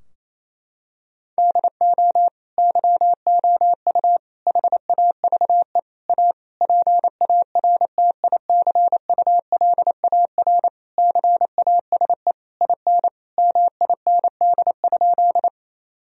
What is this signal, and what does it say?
Morse code, heard as a keyed tone, about 28 wpm, 700 Hz. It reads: DO YOU HAVE A PARTICULAR CASE IN MIND?
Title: Sentences from Top 1000 Words - Review Part 4 - 28wpm